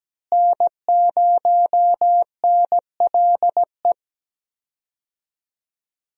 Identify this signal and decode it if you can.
Morse code, heard as a keyed tone, about 17 wpm, 700 Hz. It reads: N0NLE
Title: USA Call Signs - 5 Characters - Repeated 3x - 17wpm